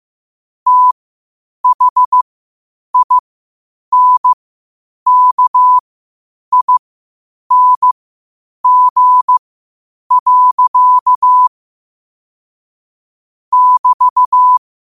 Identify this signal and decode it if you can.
Morse code, heard as a keyed tone, about 15 wpm, spaced as for 8 wpm, 1000 Hz. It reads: THINKING. <BT>
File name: qotdfiles / 8wpm